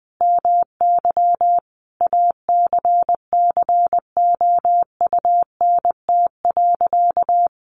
Morse code, heard as a keyed tone, about 20 wpm, 700 Hz. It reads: MY ACCOUNT.
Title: Sentences from Top 900 Words 20wpm